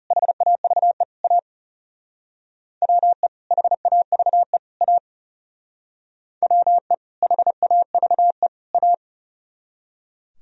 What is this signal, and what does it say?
Morse code, heard as a keyed tone, about 34 wpm, 690 Hz. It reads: HAVE A WE HAVE A WE HAVE A
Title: Speed-Racing - Sets of 3 Words 30wpm